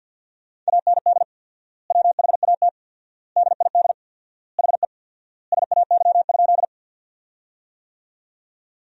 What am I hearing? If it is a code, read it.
Morse code, heard as a keyed tone, about 50 wpm, 700 Hz. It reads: AND WHAT DID HE SAY?
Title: 4x Word-Spacing - Sentences from Top 200 Words - 50wpm